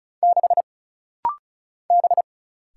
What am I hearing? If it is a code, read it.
Morse code, heard as a keyed tone, about 36 wpm, 700 Hz. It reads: 6 B
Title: Single Letter-Number - Rapid-Fire 35wpm